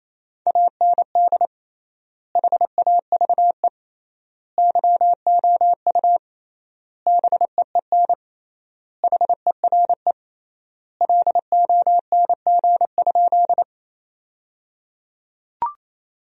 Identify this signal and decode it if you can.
Morse code, heard as a keyed tone, about 27 wpm, 700 Hz. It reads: AND HAVE YOU BEEN HERE LONG?
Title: 3x Word-Spacing - Sentences from Top 200 Words - 28wpm